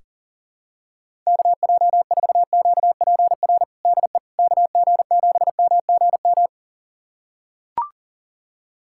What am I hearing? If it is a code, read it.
Morse code, heard as a keyed tone, about 40 wpm, 700 Hz. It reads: KJ4QPR DE KG7MGM
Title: USA Callsign DE Callsign 40wpm